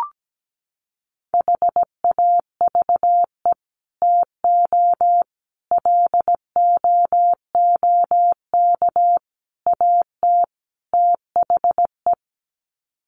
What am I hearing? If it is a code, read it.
Morse code, heard as a keyed tone, about 17 wpm, 700 Hz. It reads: HAVE TO LOOK AT THE
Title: Sets of 5 Words Encore 17wpm